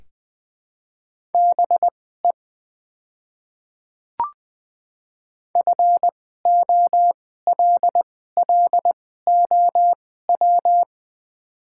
Morse code, heard as a keyed tone, about 20 wpm, spaced as for 14 wpm, 700 Hz. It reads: BE FOLLOW
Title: 2x Character Spacing - Top 200 Words - 20wpm